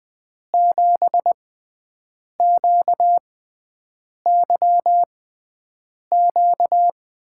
Morse code, heard as a keyed tone, about 20 wpm, spaced as for 7 wpm, 700 Hz. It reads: 7QYQ
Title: USA Call Signs - 6 Characters - 6x Character Spacing - 20wpm